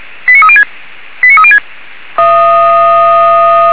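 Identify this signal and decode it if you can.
Five-tone selective call (ZVEI), heard as quick runs of five tones, tones of about 70 ms. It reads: 89306 89306
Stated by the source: ZVEI five-tone call received by SDR.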